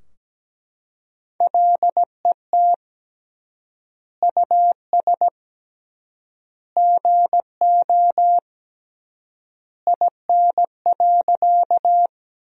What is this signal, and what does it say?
Morse code, heard as a keyed tone, about 17 wpm, 700 Hz. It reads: LET US GO IN.
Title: 3x Word-Spacing - Sentences from Top 300 Words - 17wpm